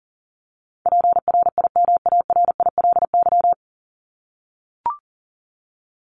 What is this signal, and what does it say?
Morse code, heard as a keyed tone, about 40 wpm, 700 Hz. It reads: PRIMARILY
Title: Top 1001-1250 Words 40wpm